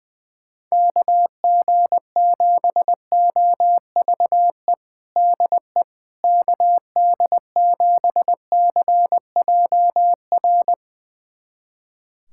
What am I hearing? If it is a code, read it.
Morse code, heard as a keyed tone, about 20 wpm, 700 Hz. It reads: KG7OVE DE KD7CJR
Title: USA Callsign DE Callsign 20wpm